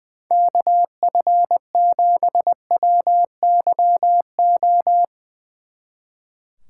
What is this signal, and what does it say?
Morse code, heard as a keyed tone, about 22 wpm, 700 Hz. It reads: KF7WYO
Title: Speed Racing - USA 2x3 Call Signs 20wpm